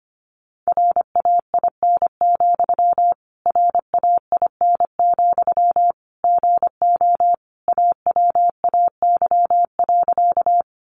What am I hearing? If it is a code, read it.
Morse code, heard as a keyed tone, about 25 wpm, 700 Hz. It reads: RAIN, RAIN, GO AWAY.
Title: Sentences from Top 600 Words 25wpm